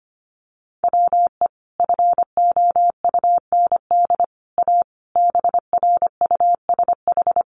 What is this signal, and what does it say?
Morse code, heard as a keyed tone, about 25 wpm, 700 Hz. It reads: WE FOUND A BRUSH
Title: Sentences from Top 1100 Words 25wpm